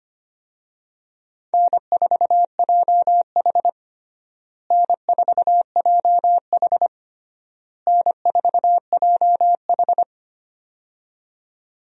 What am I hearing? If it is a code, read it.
Morse code, heard as a keyed tone, about 25 wpm, 700 Hz. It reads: N4JH N4JH N4JH
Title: USA Call Signs - 4 Characters - Repeated 3x - 25wpm